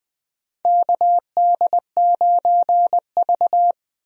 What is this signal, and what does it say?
Morse code, heard as a keyed tone, about 20 wpm, 700 Hz. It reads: KD9V
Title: USA Call Signs - 4 Characters - Repeated 2x - 20wpm